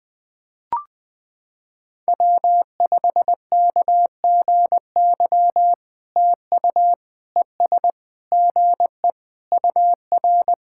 Morse code, heard as a keyed tone, about 20 wpm, 700 Hz. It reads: W5KGY TU ES GE UR
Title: POTA QSO Practice - 20wpm